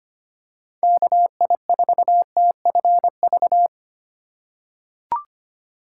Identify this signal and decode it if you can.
Morse code, heard as a keyed tone, about 25 wpm, 700 Hz. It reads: KI4TFV
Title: USA Call Signs - 6 Characters - 2x Character Spacing - 25wpm